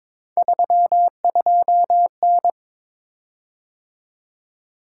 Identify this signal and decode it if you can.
Morse code, heard as a keyed tone, about 22 wpm, 700 Hz. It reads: 32N
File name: practice / 22wpm / Three Letter-Number Combinations - 22wpm